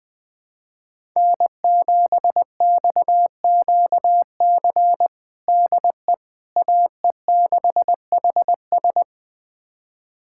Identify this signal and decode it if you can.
Morse code, heard as a keyed tone, about 20 wpm, 700 Hz. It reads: N7XQC DE AE6HS
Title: USA Callsign DE Callsign - 5 Characters - 20wpm